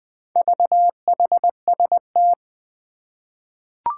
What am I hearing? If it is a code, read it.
Morse code, heard as a keyed tone, about 20 wpm, 700 Hz. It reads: VHST
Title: Four Challenging Letter-Number Combinations 20wpm